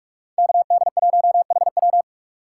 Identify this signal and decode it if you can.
Morse code, heard as a keyed tone, about 43 wpm, 700 Hz. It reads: KD1HW